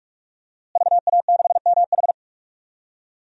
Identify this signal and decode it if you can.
Morse code, heard as a keyed tone, about 46 wpm, 700 Hz. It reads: VA6MH
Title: DXCC-100 Call Signs - 5 Characters - Repeated 3x - 45wpm